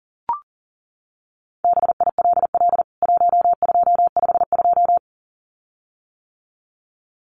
Morse code, heard as a keyed tone, about 37 wpm, 700 Hz. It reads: BILL 1252
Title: CWT Exchange 40wpm